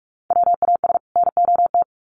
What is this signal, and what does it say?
Morse code, heard as a keyed tone, about 46 wpm, 700 Hz. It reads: WAS NOT
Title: Sets of 2 Words 45wpm